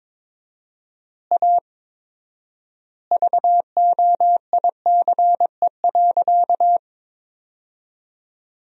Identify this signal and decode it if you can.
Morse code, heard as a keyed tone, about 22 wpm, 700 Hz. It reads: A VOICE.
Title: 4x Word-Spacing - Sentences from Top 500 Words - 22wpm